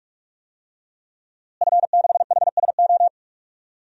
Morse code, heard as a keyed tone, about 46 wpm, 700 Hz. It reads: F6HSO
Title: DXCC-100 Call Signs - 5 Characters - Repeated 3x - 45wpm